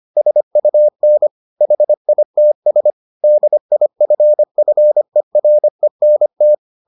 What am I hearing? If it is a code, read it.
Morse code, heard as a keyed tone, about 25 wpm, 600 Hz. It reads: SUN HITS DIFFERENT